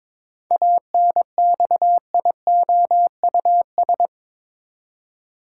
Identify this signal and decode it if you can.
Morse code, heard as a keyed tone, about 22 wpm, 700 Hz. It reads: ANXIOUS
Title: Seven Letter Words - 22wpm